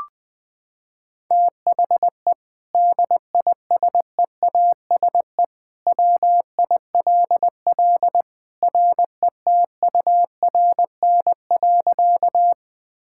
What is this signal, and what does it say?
Morse code, heard as a keyed tone, about 20 wpm, 700 Hz. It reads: THE DISEASE WILL RETURN.
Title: Sentences from Top 1000 Words - Review Part 6 - 20wpm